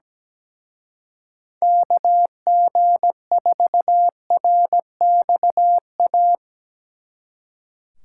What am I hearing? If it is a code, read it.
Morse code, heard as a keyed tone, about 19 wpm, 700 Hz. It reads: KG4RXA
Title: Speed Racing - USA Call Signs 17wpm